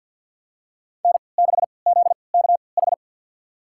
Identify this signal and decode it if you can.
Morse code, heard as a keyed tone, about 51 wpm, spaced as for 28 wpm, 700 Hz. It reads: N6ZKH
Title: USA Call Signs - 3x Character Spacing - 50wpm